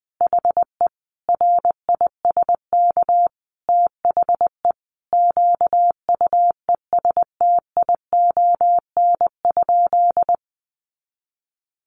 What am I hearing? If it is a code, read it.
Morse code, heard as a keyed tone, about 20 wpm, 700 Hz. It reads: HE RISK THE QUESTION?
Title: Sentences from Top 600 Words 20wpm